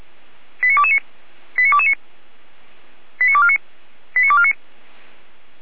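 Five-tone selective call (ZVEI), heard as quick runs of five tones, tones of about 70 ms. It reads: 89229 89229 89249 89249
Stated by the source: ZVEI five-tone selective call 89229 received on SDR.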